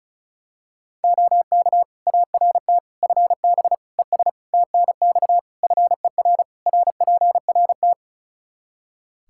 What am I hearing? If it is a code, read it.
Morse code, heard as a keyed tone, about 36 wpm, 700 Hz. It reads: OK ART FB ES TNX FER RPRT
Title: Practice QSO - Exchange 2 - Info - 35wpm